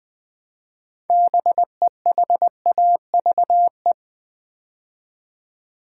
Morse code, heard as a keyed tone, about 20 wpm, 700 Hz. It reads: BEHAVE
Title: Build A Word - 3 Letters At A Time - 6-Letter Words - 20wpm